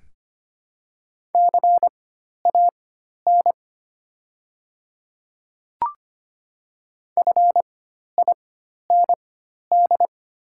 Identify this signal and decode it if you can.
Morse code, heard as a keyed tone, about 25 wpm, spaced as for 12 wpm, 690 Hz. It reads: CAN FIND
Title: 4x Character Spacing - Top 100 Words - 25wpm